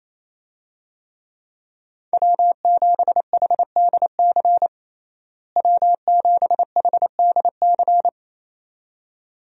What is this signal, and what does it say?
Morse code, heard as a keyed tone, about 27 wpm, 700 Hz. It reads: W7HDC W7HDC
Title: USA Call Signs - 5 Characters - Repeated 2x - 28wpm